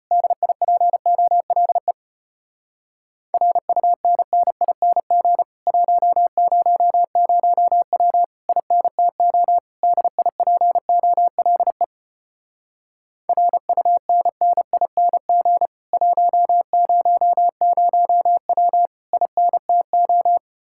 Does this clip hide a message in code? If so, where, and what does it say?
Yes: Morse code from 0.1 to 20.4 s, heard as a keyed tone, about 33 wpm, 700 Hz. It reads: DIPOLE RUNNING 100W INTO DIPOLE RUNNING 100W INTO